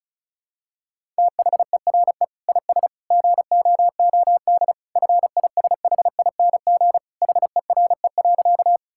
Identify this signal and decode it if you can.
Morse code, heard as a keyed tone, about 36 wpm, 700 Hz. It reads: THERE IS GOOD FISHING HERE.
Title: Sentences from Top 1000 Words - Review Part 5 - 35wpm